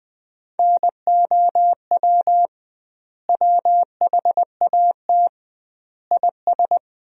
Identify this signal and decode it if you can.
Morse code, heard as a keyed tone, about 20 wpm, 700 Hz. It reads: NOW WHAT IS
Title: 2x Word-Spacing - Sentences from Top 200 Words - 20wpm